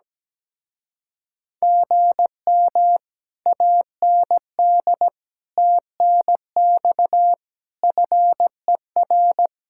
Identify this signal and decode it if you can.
Morse code, heard as a keyed tone, about 17 wpm, 700 Hz. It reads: GM AND TNX FER
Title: Speed-Racing Common QSO Phrases Encore - 17wpm